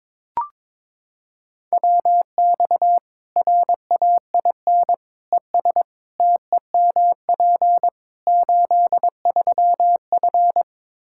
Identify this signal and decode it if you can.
Morse code, heard as a keyed tone, about 22 wpm, 700 Hz. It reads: WX RAIN ES TEMP 83F
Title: Practice QSO - Exchange 2 - Info - 22wpm